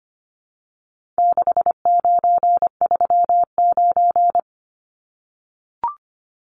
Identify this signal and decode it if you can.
Morse code, heard as a keyed tone, about 25 wpm, 700 Hz. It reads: 6939
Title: Numbers - 4-Digit 25wpm